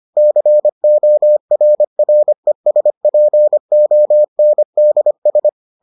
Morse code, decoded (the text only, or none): CORRESPONDS